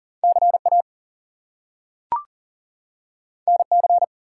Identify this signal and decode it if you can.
Morse code, heard as a keyed tone, about 40 wpm, 700 Hz. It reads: CA NC
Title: US States and Canadian Provinces 40wpm